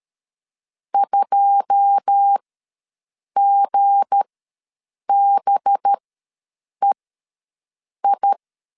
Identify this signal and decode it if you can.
Morse code, heard as a keyed tone, about 13 wpm, spaced as for 7 wpm, 800 Hz. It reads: 2GBEI
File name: LEZIONI / 10 SIGNS / 10B INT ws 12wpm 3527